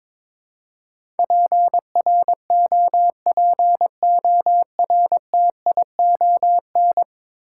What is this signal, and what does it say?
Morse code, heard as a keyed tone, about 21 wpm, 700 Hz. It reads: PROPORTION
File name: practice / 22wpm / Ten Letter Words - 22wpm